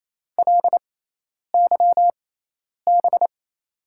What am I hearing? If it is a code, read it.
Morse code, heard as a keyed tone, about 28 wpm, spaced as for 10 wpm, 700 Hz. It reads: LYB
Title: USA Call Signs - 6 Characters - 6x Character Spacing - 28wpm